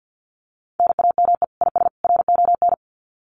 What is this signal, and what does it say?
Morse code, heard as a keyed tone, about 50 wpm, 700 Hz. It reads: NAME IS RON
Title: Common QSO Phrases 50wpm